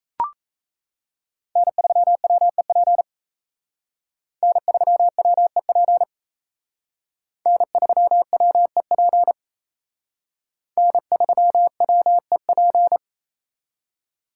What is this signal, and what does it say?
Morse code, heard as a keyed tone, about 35 wpm, 700 Hz. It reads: N3WEP N3WEP N3WEP N3WEP
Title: Speed Racing - USA Call Signs 28wpm